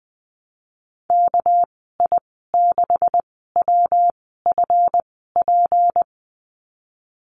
Morse code, heard as a keyed tone, about 20 wpm, spaced as for 15 wpm, 700 Hz. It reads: KI6WFP
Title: 231 Lesson 37g - Letter x - Call Signs 20wpm